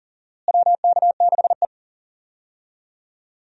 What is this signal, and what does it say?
Morse code, heard as a keyed tone, about 40 wpm, 700 Hz. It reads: WK6E
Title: USA Call Signs - 4 Characters - Repeated 3x - 40wpm